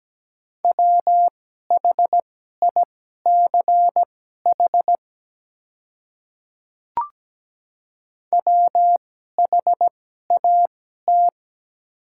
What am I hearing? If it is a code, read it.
Morse code, heard as a keyed tone, about 17 wpm, spaced as for 12 wpm, 700 Hz. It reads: WHICH WHAT
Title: 2x Character Spacing - Top 100 Words - 17wpm